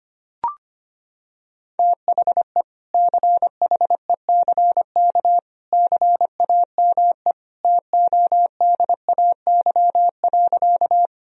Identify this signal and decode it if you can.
Morse code, heard as a keyed tone, about 25 wpm, 700 Hz. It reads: THE CHECK CAME TODAY.
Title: Sentences from Top 1000 Words - Review Part 3 - 25wpm